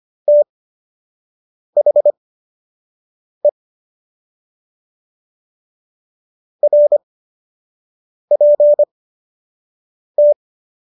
Morse code, heard as a keyed tone, about 25 wpm, spaced as for 6 wpm, 590 Hz. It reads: THE RPT